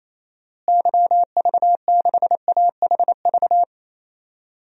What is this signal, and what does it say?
Morse code, heard as a keyed tone, about 28 wpm, 700 Hz. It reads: YV6AHV
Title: DXCC-100 Call Signs - 6 Characters - Repeated 2x - 28wpm